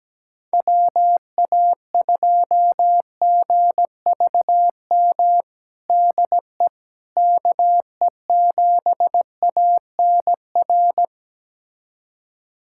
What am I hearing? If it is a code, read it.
Morse code, heard as a keyed tone, about 17 wpm, 700 Hz. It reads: WA2GVM DE KE7ANR
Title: USA Callsign DE Callsign - 6 Characters - 17wpm